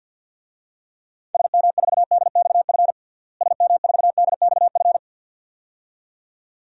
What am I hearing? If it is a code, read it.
Morse code, heard as a keyed tone, about 53 wpm, 700 Hz. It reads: SM4DXF SM4DXF
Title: DXCC-100 Call Signs - 6 Characters - Repeated 2x - 50wpm